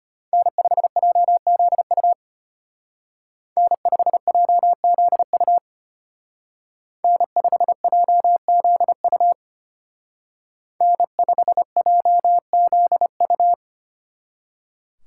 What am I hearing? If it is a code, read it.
Morse code, heard as a keyed tone, about 33 wpm, 700 Hz. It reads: N5JZU N5JZU N5JZU N5JZU